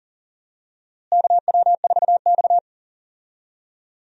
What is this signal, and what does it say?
Morse code, heard as a keyed tone, about 37 wpm, 700 Hz. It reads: KW4X
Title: USA Call Signs - 4 Characters - Repeated 2x - 40wpm